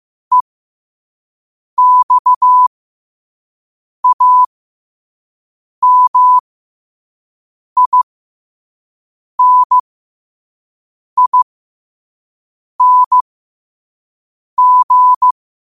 Morse code, heard as a keyed tone, about 15 wpm, spaced as for 5 wpm, 1000 Hz. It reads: EXAMINING